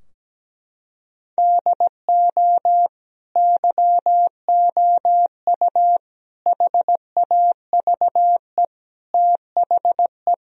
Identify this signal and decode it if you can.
Morse code, heard as a keyed tone, about 17 wpm, 700 Hz. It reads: DO YOU HAVE THE
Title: Sentences from Top 1000 Words - Review Part 2 - 17wpm